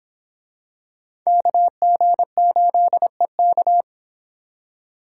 Morse code, heard as a keyed tone, about 26 wpm, 700 Hz. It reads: KG8EK